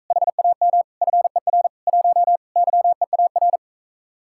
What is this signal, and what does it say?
Morse code, heard as a keyed tone, about 45 wpm, 700 Hz. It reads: HAM FER 1 YEAR